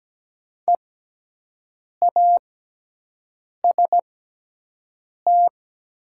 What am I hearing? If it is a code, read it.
Morse code, heard as a keyed tone, about 17 wpm, spaced as for 6 wpm, 700 Hz. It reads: EAST
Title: 6x Character Spacing - Top 500 Words - 17wpm